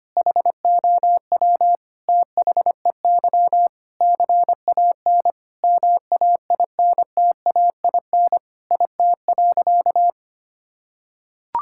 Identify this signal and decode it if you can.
Morse code, heard as a keyed tone, about 25 wpm, 700 Hz. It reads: HOW THEY CAN MAINTAIN IT.
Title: Sentences from Top 1000 Words - Review Part 5 - 25wpm